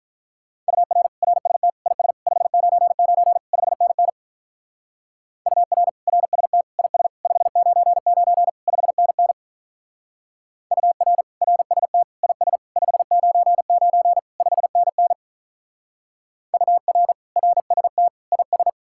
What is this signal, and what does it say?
Morse code, heard as a keyed tone, about 44 wpm, 700 Hz. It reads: UR RST IS 599 5NN UR RST IS 599 5NN UR RST IS 599 5NN UR RST IS